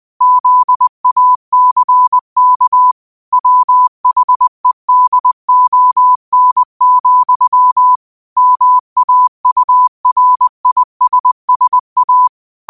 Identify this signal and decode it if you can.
Morse code, heard as a keyed tone, about 20 wpm, 1000 Hz. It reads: ZACK WHEDON, MAURISSA
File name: 20wpm